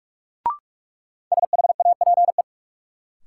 Morse code, heard as a keyed tone, about 46 wpm, 700 Hz. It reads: SHAPE
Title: Sets of 5 Words Encore - New Word Drill - 45wpm